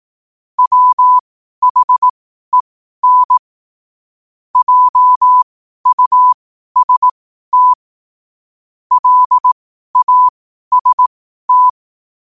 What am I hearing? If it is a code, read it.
Morse code, heard as a keyed tone, about 18 wpm, spaced as for 12 wpm, 1000 Hz. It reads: WHEN JUST LAST
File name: qotdfiles / 12wpm